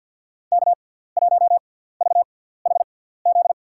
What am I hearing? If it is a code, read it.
Morse code, heard as a keyed tone, about 52 wpm, spaced as for 17 wpm, 700 Hz. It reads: K1VHZ
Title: USA Call Signs - 5 Characters - 6x Character Spacing - 50wpm